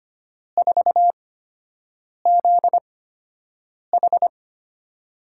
Morse code, heard as a keyed tone, about 25 wpm, spaced as for 7 wpm, 700 Hz. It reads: 4ZH
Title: USA Call Signs - 5 Characters - 8x Character Spacing - 25wpm